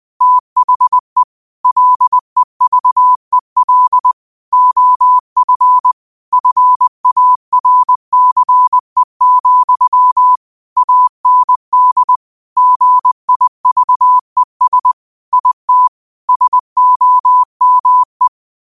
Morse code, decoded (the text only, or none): THE LEVEL OF FARCE, AND GIVES IT SOME